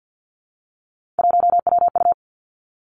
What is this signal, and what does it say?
Morse code, heard as a keyed tone, about 50 wpm, 700 Hz. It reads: 1WU
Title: Three Challenging Letter-Number Combinations 50wpm